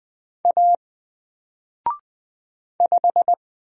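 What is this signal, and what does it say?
Morse code, heard as a keyed tone, about 20 wpm, 700 Hz. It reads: A 5
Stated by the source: Single letter and number Morse drill at 20 wpm, from N4LCD.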